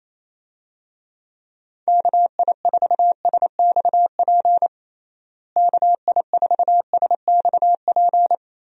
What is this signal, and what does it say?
Morse code, heard as a keyed tone, about 28 wpm, 700 Hz. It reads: KI4SXP KI4SXP
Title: USA Call Signs - 6 Characters - Repeated 3x - 28wpm